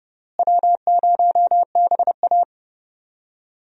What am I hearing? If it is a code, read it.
Morse code, heard as a keyed tone, about 30 wpm, 700 Hz. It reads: W0BA
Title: USA Call Signs - 4 Characters - Repeated 2x - 30wpm